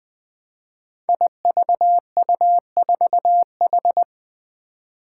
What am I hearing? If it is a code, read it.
Morse code, heard as a keyed tone, about 20 wpm, 700 Hz. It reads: IVU4H
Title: Five Challenging Letter-Number Combinations 20wpm